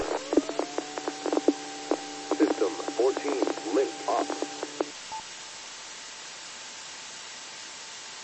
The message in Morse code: none

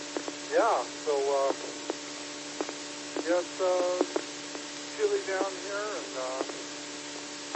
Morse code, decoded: none

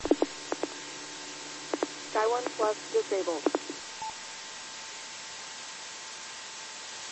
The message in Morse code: none

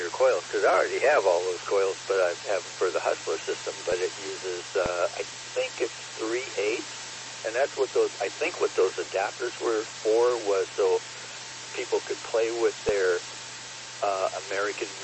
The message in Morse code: none

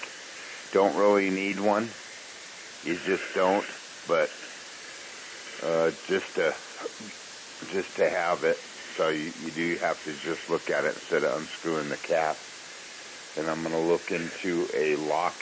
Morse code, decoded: none

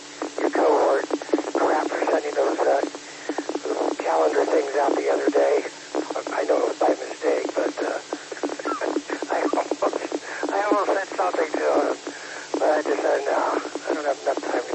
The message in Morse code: none

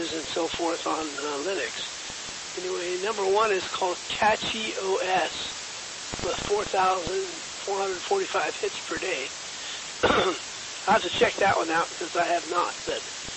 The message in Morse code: none

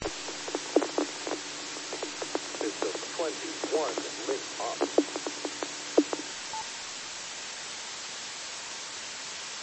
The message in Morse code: none

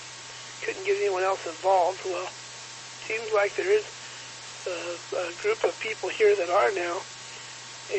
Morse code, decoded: none

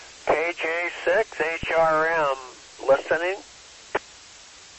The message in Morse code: none

